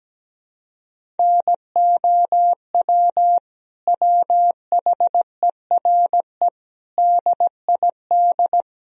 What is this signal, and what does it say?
Morse code, heard as a keyed tone, about 17 wpm, 700 Hz. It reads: NOW WHERE DID
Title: Sentences from Top 200 Words and Abbreviations 17wpm